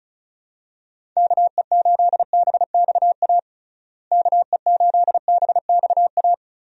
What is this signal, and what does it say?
Morse code, heard as a keyed tone, about 34 wpm, 700 Hz. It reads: KE8BXA KE8BXA
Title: USA Call Signs - 6 Characters - Repeated 3x - 35wpm